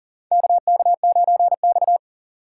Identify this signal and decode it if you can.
Morse code, heard as a keyed tone, about 40 wpm, 700 Hz. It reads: KK9X
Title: USA Call Signs - 2x Character Spacing - 40wpm